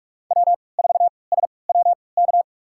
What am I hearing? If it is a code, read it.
Morse code, heard as a keyed tone, about 42 wpm, spaced as for 25 wpm, 700 Hz. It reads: W4SWK
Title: USA Call Signs - 3x Character Spacing - 45wpm